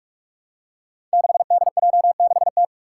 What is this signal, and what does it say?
Morse code, heard as a keyed tone, about 46 wpm, 700 Hz. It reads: 6DJ6T
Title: Five Challenging Letter-Number Combinations 45wpm